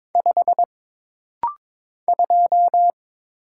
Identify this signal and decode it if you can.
Morse code, heard as a keyed tone, about 22 wpm, 700 Hz. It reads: 5 2
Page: This rapid-fire timing is geared to push your ICR (Instant Character Recognition) skills.